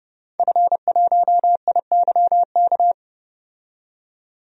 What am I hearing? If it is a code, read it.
Morse code, heard as a keyed tone, about 30 wpm, 700 Hz. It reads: F1IYK
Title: DXCC-100 Call Signs - 5 Characters - 30wpm